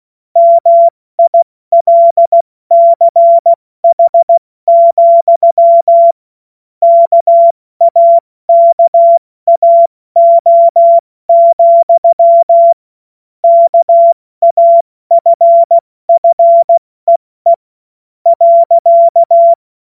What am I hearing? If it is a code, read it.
Morse code, heard as a keyed tone, about 16 wpm, spaced as for 13 wpm, 680 Hz. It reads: MILCH, KAKAO, KAFFEE.
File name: Morsekurs / Witze / Witz 01. Februar 2022 70 F80